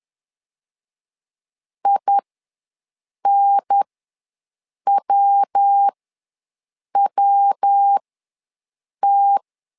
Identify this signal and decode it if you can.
Morse code, heard as a keyed tone, about 10 wpm, spaced as for 6 wpm, 800 Hz. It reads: INWWT
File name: LEZIONI / 3 MWRO / 3 MW ws 10wpm 2213